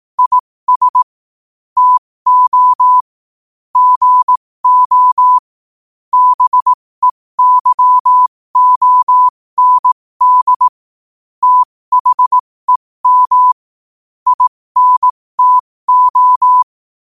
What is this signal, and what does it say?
Morse code, heard as a keyed tone, about 18 wpm, spaced as for 15 wpm, 1000 Hz. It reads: IS TO GO BEYOND THEM INTO
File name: qotdfiles / 15wpm